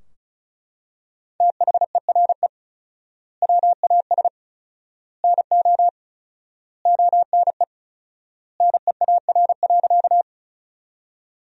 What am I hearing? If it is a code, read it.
Morse code, heard as a keyed tone, about 34 wpm, 700 Hz. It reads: THERE WAS NO ONE NEAR.
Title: 4x Word-Spacing - Sentences from Top 200 Words - 35wpm